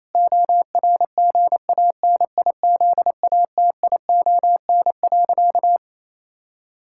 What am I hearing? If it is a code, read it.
Morse code, heard as a keyed tone, about 28 wpm, 700 Hz. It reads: ORGANIZATION.
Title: Sentences from Top 1000 Words - Review Part 6 - 28wpm